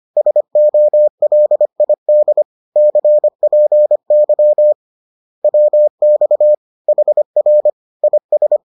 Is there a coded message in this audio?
Yes: Morse code from 0.2 to 8.6 s, heard as a keyed tone, about 25 wpm, 600 Hz. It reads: SOLID CPY WX HR IS